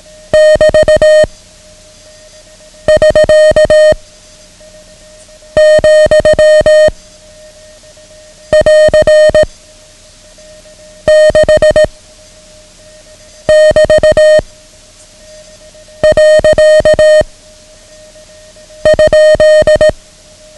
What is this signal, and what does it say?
Morse code, heard as a keyed tone, about 18 wpm, 620 Hz. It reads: <BT> <SK> , <AR> 6 <BT> . ?
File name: ham radio / CodeQuick / Code Quick 09